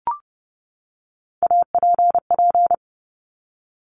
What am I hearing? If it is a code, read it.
Morse code, heard as a keyed tone, about 30 wpm, 700 Hz. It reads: APP
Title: Top Three Letter Combinations in Words 30wpm